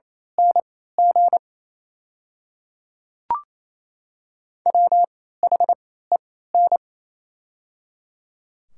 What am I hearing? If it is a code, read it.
Morse code, heard as a keyed tone, about 28 wpm, spaced as for 16 wpm, 700 Hz. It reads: NG WHEN
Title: 3x Character Spacing - Top 200 Words - 28wpm